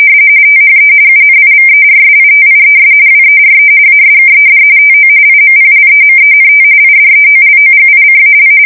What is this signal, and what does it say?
RTTY, about 45.45 baud. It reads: THE UNIVERSITIES WERE ASKED TO IMPLEMENT IDEOLOGICAL
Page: 170 Shift 2125/2295